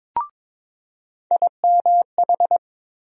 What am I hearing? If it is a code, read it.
Morse code, heard as a keyed tone, about 22 wpm, 700 Hz. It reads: IMH
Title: USA Call Sign Postfixes - 3 Letters 22wpm